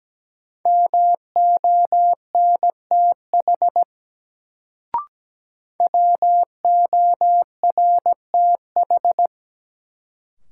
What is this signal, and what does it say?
Morse code, heard as a keyed tone, about 17 wpm, 700 Hz. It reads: MONTH WORTH
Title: Sentences from Top 800 Words - New Word Drill - 17wpm